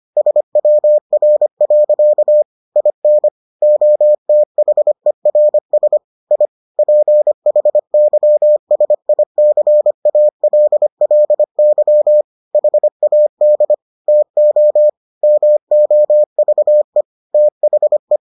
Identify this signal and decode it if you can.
Morse code, heard as a keyed tone, about 25 wpm, 600 Hz. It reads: SWR. IN OTHERS I PHYSICALLY HAD TO MOVE THE